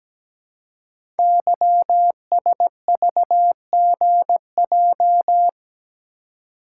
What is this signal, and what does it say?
Morse code, heard as a keyed tone, about 17 wpm, 700 Hz. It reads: YSVGJ
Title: Code Groups - 5 Letters - 17wpm